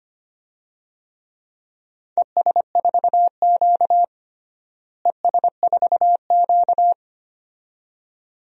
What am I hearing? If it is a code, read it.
Morse code, heard as a keyed tone, about 25 wpm, 700 Hz. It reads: ES4Q ES4Q
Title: DXCC-100 Call Signs - 4 Characters - Repeated 2x - 25wpm